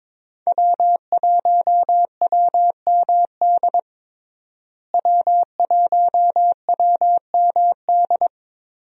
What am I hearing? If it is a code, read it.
Morse code, heard as a keyed tone, about 21 wpm, 700 Hz. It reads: W1WMD W1WMD
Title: USA Call Signs - 5 Characters - Repeated 3x - 22wpm